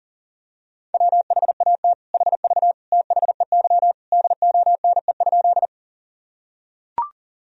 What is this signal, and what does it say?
Morse code, heard as a keyed tone, about 37 wpm, 700 Hz. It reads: WHAT HV THEY DONE?
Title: Sentences from Top 500 Words and Abbreviations 40wpm